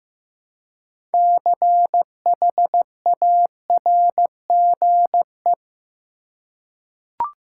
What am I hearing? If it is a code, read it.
Morse code, heard as a keyed tone, about 15 wpm, 700 Hz. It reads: CHARGE
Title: Six Letter Words - 15wpm